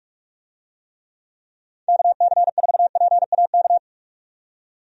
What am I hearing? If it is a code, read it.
Morse code, heard as a keyed tone, about 46 wpm, 700 Hz. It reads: KC4PAK
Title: USA Call Signs - 6 Characters - 45wpm